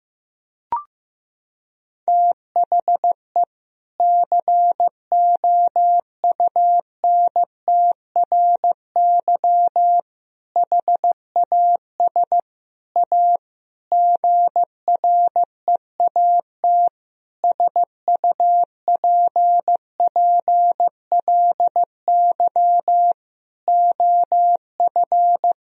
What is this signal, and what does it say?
Morse code, heard as a keyed tone, about 15 wpm, 700 Hz. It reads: THE COUNTRY HAS A GREAT SUPPLY OF
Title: Sentences from Top 1000 Words - Review Part 4 - 15wpm